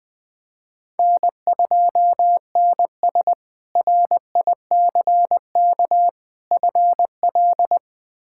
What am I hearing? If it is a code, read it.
Morse code, heard as a keyed tone, about 20 wpm, 700 Hz. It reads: N2NS RICK FL
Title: K1USN SST QSO Practice - 20wpm